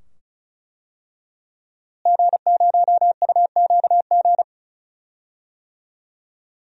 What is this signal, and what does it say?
Morse code, heard as a keyed tone, about 34 wpm, 700 Hz. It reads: G0UQG
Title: DXCC-100 Call Signs - 5 Characters - Repeated 2x - 35wpm